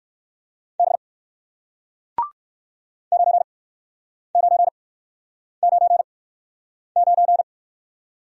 Morse code, heard as a keyed tone, about 61 wpm, spaced as for 9 wpm, 700 Hz. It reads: B 9999